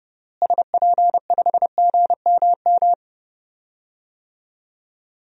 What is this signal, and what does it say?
Morse code, heard as a keyed tone, about 30 wpm, 700 Hz. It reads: SP5GMM
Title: DXCC-100 Call Signs - 6 Characters - Repeated 2x - 30wpm